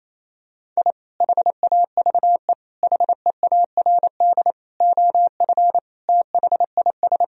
Morse code, heard as a keyed tone, about 28 wpm, 700 Hz. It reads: I HAVE HEARD OF THIS
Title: Sentences from Top 1000 Words - Review Part 1 - 28wpm